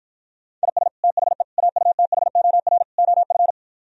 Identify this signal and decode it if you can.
Morse code, heard as a keyed tone, about 53 wpm, 700 Hz. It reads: IS THE AUTHOR OF